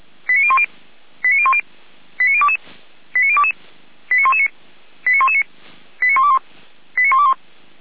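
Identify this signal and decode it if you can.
Five-tone selective call (ZVEI), heard as quick runs of five tones, tones of about 70 ms. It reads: 89010 89010 89022 89022 89119 89119 89121 89121
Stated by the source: ZVEI five-tone call 89011 received by SDR.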